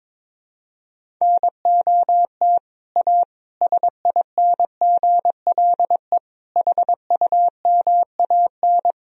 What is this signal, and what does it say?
Morse code, heard as a keyed tone, about 22 wpm, 700 Hz. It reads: NOT A SINGLE HUMAN
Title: Sentences from Top 1000 Words - Review Part 4 - 22wpm